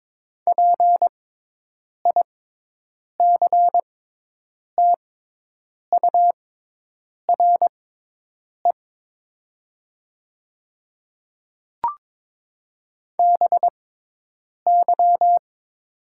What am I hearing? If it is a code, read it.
Morse code, heard as a keyed tone, about 22 wpm, spaced as for 8 wpm, 700 Hz. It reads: PICTURE BY
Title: 6x Character Spacing - Top 200 Words - 22wpm